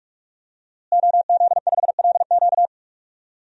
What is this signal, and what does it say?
Morse code, heard as a keyed tone, about 42 wpm, 700 Hz. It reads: OZ5LQ
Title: DXCC-100 Call Signs - 5 Characters - Repeated 3x - 45wpm